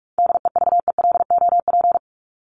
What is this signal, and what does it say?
Morse code, heard as a keyed tone, about 44 wpm, 700 Hz. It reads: DEVELOP
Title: ING Suffix - Top 100 Words with Base Words 45wpm